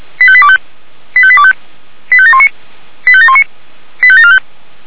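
Five-tone selective call (ZVEI), heard as quick runs of five tones, tones of about 70 ms. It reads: 85825 85825 85719 85719 85745
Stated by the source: ZVEI five-tone call received on SDR.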